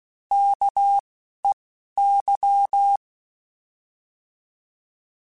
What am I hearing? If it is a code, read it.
Morse code, heard as a keyed tone, about 16 wpm, 800 Hz. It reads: K E Y